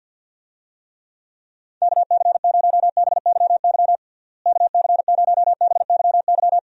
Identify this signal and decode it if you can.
Morse code, heard as a keyed tone, about 48 wpm, 700 Hz. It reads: KC0BYY KC0BYY
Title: USA Call Signs - 6 Characters - Repeated 2x - 50wpm